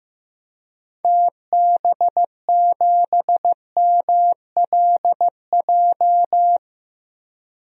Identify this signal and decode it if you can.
Morse code, heard as a keyed tone, about 15 wpm, 700 Hz. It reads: TB7MLJ
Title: DXCC-100 Call Signs - 6 Characters - 15wpm